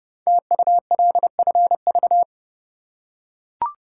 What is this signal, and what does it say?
Morse code, heard as a keyed tone, about 29 wpm, 690 Hz. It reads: TULFV
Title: Code Groups - 5 Letters - 30wpm